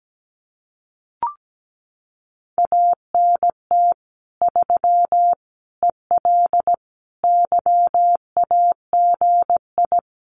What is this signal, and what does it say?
Morse code, heard as a keyed tone, about 17 wpm, 700 Hz. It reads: ANT 3 EL YAGI
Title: Common QSO Phrases Encore 17wpm